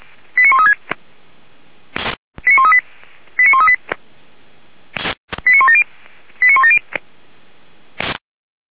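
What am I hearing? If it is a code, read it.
Five-tone selective call (ZVEI), heard as quick runs of five tones, tones of about 70 ms. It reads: 89136 89138 89138 89170 89170
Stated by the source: Five-tone ZVEI selective call received by SDR, from the audio server of Feuerwehr Orlamünde (fire brigade).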